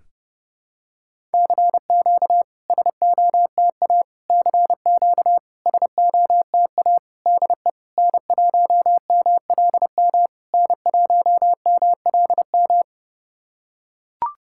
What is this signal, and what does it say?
Morse code, heard as a keyed tone, about 30 wpm, 690 Hz. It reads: CQ SOTA CQ SOTA DE N1MLM N1MLM
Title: SOTA QSO Practice 30wpm